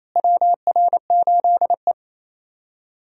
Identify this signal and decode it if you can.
Morse code, heard as a keyed tone, about 28 wpm, 700 Hz. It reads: WR8E
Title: USA Call Signs - 4 Characters - Repeated 3x - 28wpm